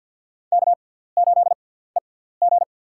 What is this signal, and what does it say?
Morse code, heard as a keyed tone, about 53 wpm, spaced as for 17 wpm, 690 Hz. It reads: K8EG
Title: USA Call Signs - 4 Characters - 6x Character Spacing - 50wpm